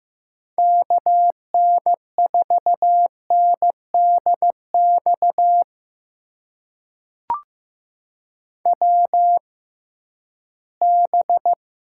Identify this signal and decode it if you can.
Morse code, heard as a keyed tone, about 15 wpm, spaced as for 11 wpm, 700 Hz. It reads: KN4NDX WB
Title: USA Call Signs - 6 Characters - 6x Character Spacing - 15wpm